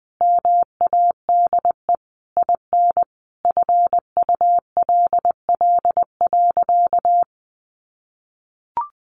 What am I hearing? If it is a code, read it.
Morse code, heard as a keyed tone, about 20 wpm, 700 Hz. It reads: MADE IN FULL.
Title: Sentences from Top 1200 Words 20wpm